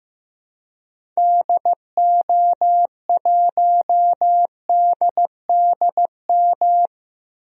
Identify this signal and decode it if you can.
Morse code, heard as a keyed tone, about 16 wpm, 700 Hz. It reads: DO1DDM
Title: DXCC-100 Call Signs - 6 Characters - 15wpm